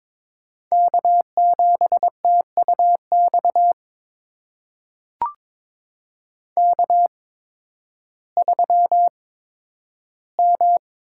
Morse code, heard as a keyed tone, about 22 wpm, spaced as for 12 wpm, 700 Hz. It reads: K7TUX K3M
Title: USA Call Signs - 5 Characters - 8x Character Spacing - 22wpm